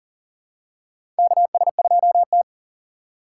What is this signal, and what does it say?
Morse code, heard as a keyed tone, about 40 wpm, 700 Hz. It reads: KS2T
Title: USA Call Signs - 4 Characters - 6x Character Spacing - 40wpm